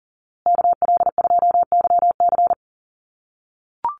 Morse code, heard as a keyed tone, about 40 wpm, 700 Hz. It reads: KL2YC
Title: USA Call Signs Repeated 3x 40wpm